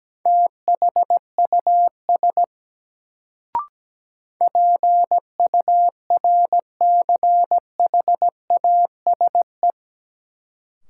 Morse code, heard as a keyed tone, about 17 wpm, 700 Hz. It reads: THUS PURCHASE
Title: Sentences from Top 800 Words - New Word Drill - 17wpm